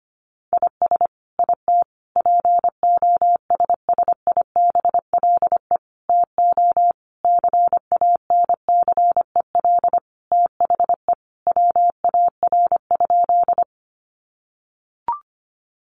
Morse code, heard as a keyed tone, about 25 wpm, 700 Hz. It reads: IS IT POSSIBLE TO CANCEL THE WAR?
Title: Sentences from Top 1200 Words 25wpm